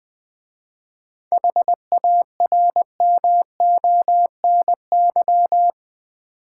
Morse code, heard as a keyed tone, about 20 wpm, 700 Hz. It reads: HARMONY
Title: 7 Letter Words - First 3 Letters Given - 20wpm